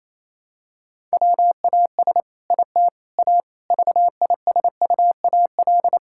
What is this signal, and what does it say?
Morse code, heard as a keyed tone, about 28 wpm, 700 Hz. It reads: WAS IT A VISUAL
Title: Sentences from Top 1000 Words - Review Part 6 - 28wpm